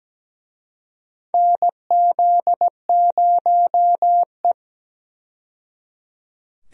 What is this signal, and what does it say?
Morse code, heard as a keyed tone, about 17 wpm, 700 Hz. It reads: NZ0E
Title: USA Call Signs - 4 Characters - 17wpm